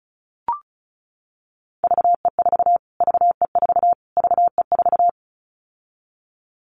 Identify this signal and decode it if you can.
Morse code, heard as a keyed tone, about 34 wpm, 700 Hz. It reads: VE4 VE4 VE4
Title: Callsign Prefixes for Top 25 DXCC - Repeated 3x - 35wpm